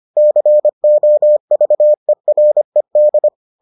Morse code, heard as a keyed tone, about 25 wpm, 600 Hz. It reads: COVERED